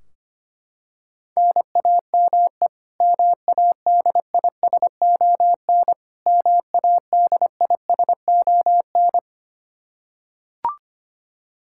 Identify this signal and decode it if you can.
Morse code, heard as a keyed tone, about 25 wpm, 700 Hz. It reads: NAME MADISON MADISON
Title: K1USN SST QSO Practice 25wpm